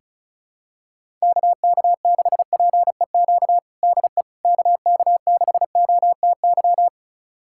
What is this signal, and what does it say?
Morse code, heard as a keyed tone, about 36 wpm, 700 Hz. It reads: KK6PEQ DE KK6OTY
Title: USA Callsign DE Callsign - 6 Characters - 35wpm